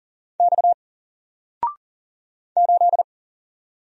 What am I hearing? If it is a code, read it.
Morse code, heard as a keyed tone, about 42 wpm, 700 Hz. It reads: X 8
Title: Single Character (Letter:Number:Punctuation:Prosign) 40wpm